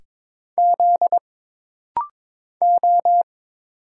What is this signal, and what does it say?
Morse code, heard as a keyed tone, about 22 wpm, 700 Hz. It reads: Z O
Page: This rapid-fire timing is geared to push your ICR (Instant Character Recognition) skills.